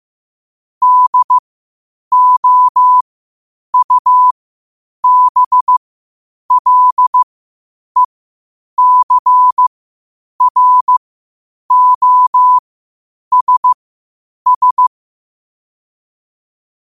Morse code, heard as a keyed tone, about 15 wpm, spaced as for 8 wpm, 1000 Hz. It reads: DOUBLECROSS
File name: qotdfiles / 8wpm